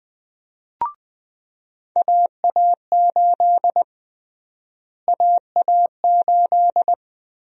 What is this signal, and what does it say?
Morse code, heard as a keyed tone, about 20 wpm, 700 Hz. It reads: AA8 AA8
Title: USA Call Sign Prefixes - Repeated 2x - 20wpm